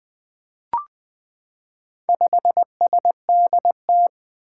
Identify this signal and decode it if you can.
Morse code, heard as a keyed tone, about 20 wpm, 700 Hz. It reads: 5SDT
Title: Four Challenging Letter-Number Combinations 20wpm